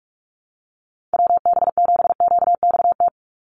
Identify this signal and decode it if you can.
Morse code, heard as a keyed tone, about 44 wpm, 700 Hz. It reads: WB7QXT
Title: USA Call Signs 45wpm